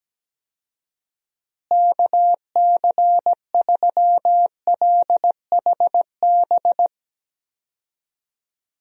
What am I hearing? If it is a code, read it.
Morse code, heard as a keyed tone, about 17 wpm, 700 Hz. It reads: KC3LHB
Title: USA Call Signs - 6 Characters - Repeated 3x - 17wpm